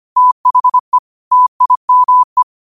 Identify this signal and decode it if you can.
Morse code, heard as a keyed tone, about 25 wpm, 1000 Hz. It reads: THE TIME
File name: qotdfiles / 25wpm